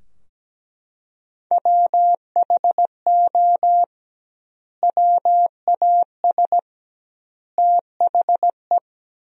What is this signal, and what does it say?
Morse code, heard as a keyed tone, about 17 wpm, 700 Hz. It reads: WHO WAS THE
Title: 2x Word-Spacing - Sentences from Top 300 Words - 17wpm